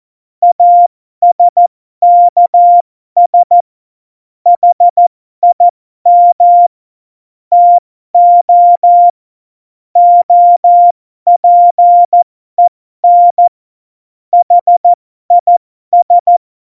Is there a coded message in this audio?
Yes: Morse code from 0.4 to 16.4 s, heard as a keyed tone, about 14 wpm, spaced as for 12 wpm, 700 Hz. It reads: ASKS HIM TO OPEN HIS